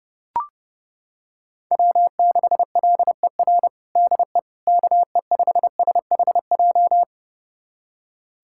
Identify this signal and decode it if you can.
Morse code, heard as a keyed tone, about 30 wpm, 700 Hz. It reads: W6LER DE KE5SHJ
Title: USA Callsign DE Callsign 30wpm